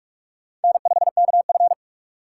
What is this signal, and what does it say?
Morse code, heard as a keyed tone, about 42 wpm, 700 Hz. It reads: N5KF
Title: USA Call Signs - 4 Characters - 2x Character Spacing - 45wpm